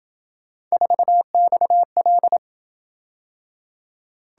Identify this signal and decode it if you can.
Morse code, heard as a keyed tone, about 27 wpm, 700 Hz. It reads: 4XL